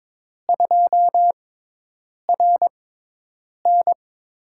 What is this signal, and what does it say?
Morse code, heard as a keyed tone, about 22 wpm, spaced as for 8 wpm, 700 Hz. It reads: 2RN
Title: USA Call Signs - 6x Character Spacing - 22wpm